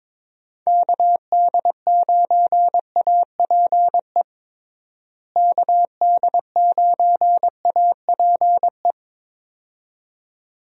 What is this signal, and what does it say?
Morse code, heard as a keyed tone, about 21 wpm, 700 Hz. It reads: KD9APE KD9APE
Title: USA Call Signs - 6 Characters - Repeated 2x - 22wpm